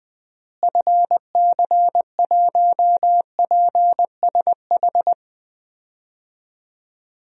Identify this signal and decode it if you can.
Morse code, heard as a keyed tone, about 20 wpm, 700 Hz. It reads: FC1PSH
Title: DXCC-100 Call Signs - 6 Characters - Repeated 2x - 20wpm